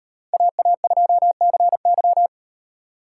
Morse code, heard as a keyed tone, about 36 wpm, 700 Hz. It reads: AA2CY